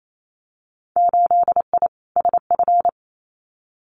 Morse code, heard as a keyed tone, about 28 wpm, 700 Hz. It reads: 8I SF
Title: Winter Field Day Exchanges Encore 28wpm